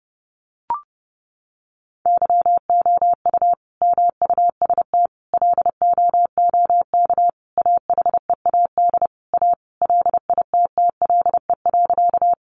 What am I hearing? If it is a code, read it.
Morse code, heard as a keyed tone, about 30 wpm, 700 Hz. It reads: YOU MUST LOOK AHEAD A LITTLE.
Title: Sentences from Top 1100 Words 30wpm